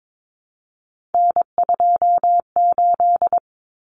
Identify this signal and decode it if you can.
Morse code, heard as a keyed tone, about 22 wpm, 700 Hz. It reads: N28
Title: Common Cut Numbers - 3-Digit 22wpm